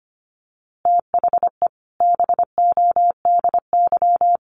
Morse code, heard as a keyed tone, about 25 wpm, 700 Hz. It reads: THE BODY
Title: A Sound Of Thunder 25wpm